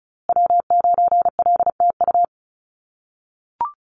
Morse code, heard as a keyed tone, about 34 wpm, 700 Hz. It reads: W9LTU
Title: USA Call Signs Repeated 3x 35wpm